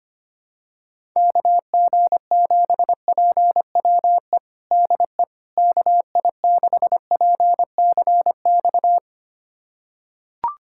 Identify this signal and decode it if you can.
Morse code, heard as a keyed tone, about 25 wpm, 700 Hz. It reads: KG7PWE DE KI6PCX
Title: USA Callsign DE Callsign - 6 Characters - 25wpm